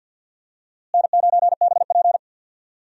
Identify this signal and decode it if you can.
Morse code, heard as a keyed tone, about 45 wpm, 700 Hz. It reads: N9BP
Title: USA Call Signs - 4 Characters - Repeated 2x - 50wpm